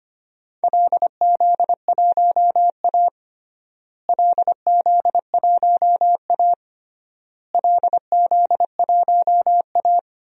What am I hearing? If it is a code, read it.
Morse code, heard as a keyed tone, about 25 wpm, 700 Hz. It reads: LZ1A LZ1A LZ1A
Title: DXCC-100 Call Signs - 4 Characters - Repeated 3x - 25wpm